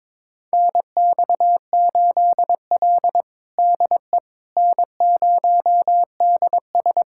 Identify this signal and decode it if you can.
Morse code, heard as a keyed tone, about 22 wpm, 700 Hz. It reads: NX8L DE N0DS
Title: USA Callsign DE Callsign - 4 Characters - 22wpm